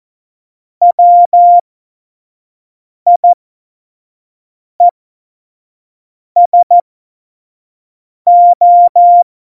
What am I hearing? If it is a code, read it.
Morse code, heard as a keyed tone, about 14 wpm, spaced as for 5 wpm, 700 Hz. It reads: WIESO
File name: Morsekurs / Witze / Witz 17. September 2019, 25 F70